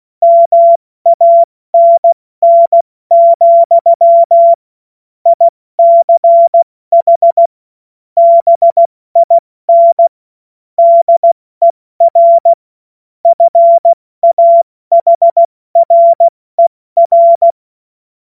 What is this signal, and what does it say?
Morse code, heard as a keyed tone, about 16 wpm, 680 Hz. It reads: MANN, ICH BIN DER FAHRER
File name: Morsekurs / Witze / Witz 03 Juni Mai 2025 70 F80